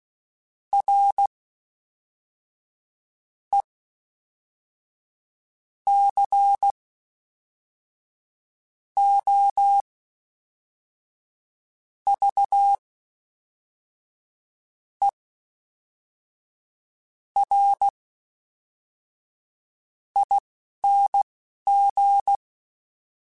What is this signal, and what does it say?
Morse code, heard as a keyed tone, about 16 wpm, spaced as for 4 wpm, 800 Hz. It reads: RECOVERING
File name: Special INGs / Special INGs-mp3s / Fast ing @10